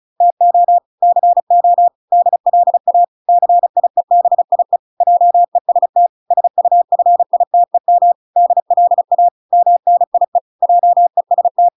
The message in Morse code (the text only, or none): TO CO DLA CIEBIE JEST SUFITEM DLA MNIE JEST